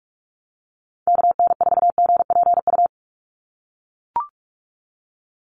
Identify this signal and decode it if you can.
Morse code, heard as a keyed tone, about 44 wpm, 700 Hz. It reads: KN4GPU
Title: USA Call Signs 45wpm